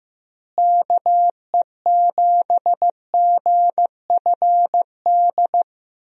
Morse code, heard as a keyed tone, about 15 wpm, 700 Hz. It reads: KE7GFD
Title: USA Call Signs - 6 Characters - Repeated 3x - 15wpm